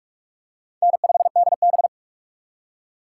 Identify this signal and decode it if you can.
Morse code, heard as a keyed tone, about 44 wpm, 700 Hz. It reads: N5DB